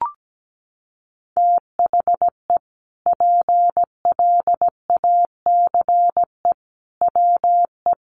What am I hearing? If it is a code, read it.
Morse code, heard as a keyed tone, about 17 wpm, 700 Hz. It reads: THE PLACE WE
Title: Sentences from Top 1400 Words 17wpm